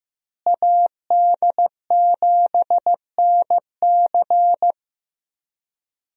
Morse code, heard as a keyed tone, about 15 wpm, 700 Hz. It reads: AD7NC
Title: USA Call Signs - 5 Characters - Repeated 3x - 15wpm